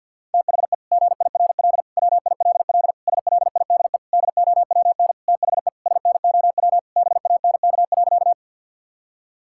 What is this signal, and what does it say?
Morse code, heard as a keyed tone, about 50 wpm, 700 Hz. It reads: THE GIRL WILL SLIDE DOWN THE SNOW BANK.
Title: Sentences from Top 1600 Words 50wpm